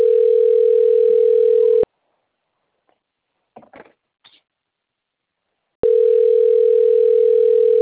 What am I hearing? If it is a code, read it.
Ringback tone (North America), 440 + 480 Hz.
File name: rg-1600-unknown- - - .47785